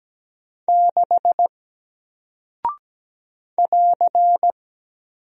Morse code, heard as a keyed tone, about 17 wpm, 700 Hz. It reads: 6 <AR>
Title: Single Character (Letter:Number:Punctuation:Prosign) Alt 17wpm